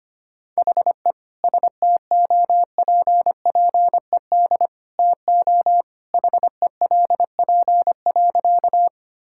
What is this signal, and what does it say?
Morse code, heard as a keyed tone, about 25 wpm, 700 Hz. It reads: HE STOPPED TO HELP.
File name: practice / 25wpm / Sentences with 2-Word Phrases - 25wpm